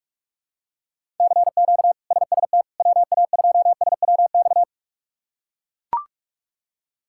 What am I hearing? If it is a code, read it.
Morse code, heard as a keyed tone, about 41 wpm, 700 Hz. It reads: CQ SST WA2SWX
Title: K1USN SST QSO Practice - 45wpm